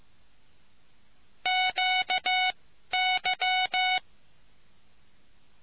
Morse code, heard as a keyed tone, about 15 wpm, spaced as for 12 wpm, 740 Hz. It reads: QY